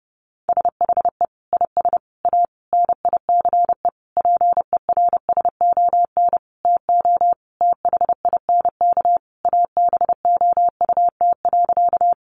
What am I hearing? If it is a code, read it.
Morse code, heard as a keyed tone, about 30 wpm, 700 Hz. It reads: SHE IS A NICE PERSON TO THINK ABOUT.
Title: Sentences from Top 1100 Words 30wpm